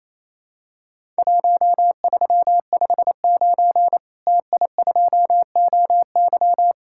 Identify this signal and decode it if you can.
Morse code, heard as a keyed tone, about 28 wpm, 700 Hz. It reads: 1359 TI2OY
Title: Worked All Europe DX Contest - QTC Exchange - 28wpm